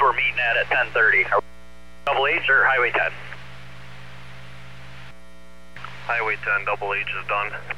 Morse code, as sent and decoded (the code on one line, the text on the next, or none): none